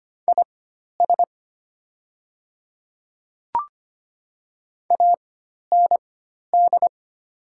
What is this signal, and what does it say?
Morse code, heard as a keyed tone, about 25 wpm, spaced as for 12 wpm, 700 Hz. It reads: IS AND
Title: 4x Character Spacing - Top 100 Words - 25wpm